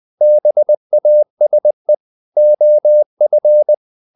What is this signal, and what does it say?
Morse code, heard as a keyed tone, about 20 wpm, 600 Hz. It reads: BASE OF
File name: the old man and the sea part 2 20wpm